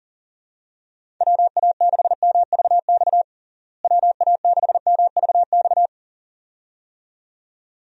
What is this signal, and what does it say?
Morse code, heard as a keyed tone, about 40 wpm, 700 Hz. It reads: WA6MVX WA6MVX
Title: USA Call Signs - 6 Characters - Repeated 2x - 40wpm